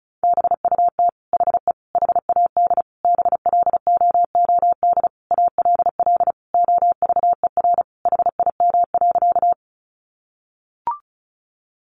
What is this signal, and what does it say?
Morse code, heard as a keyed tone, about 34 wpm, 700 Hz. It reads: BUT HE HAD BLOOD ALL OVER HIM.
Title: Sentences from Top 1100 Words 35wpm